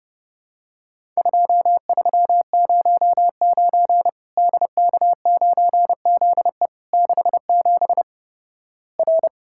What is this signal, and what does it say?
Morse code, heard as a keyed tone, about 30 wpm, 700 Hz. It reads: 2309 DK9ZE 67 R
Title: Worked All Europe DX Contest - QTC Exchange - 30wpm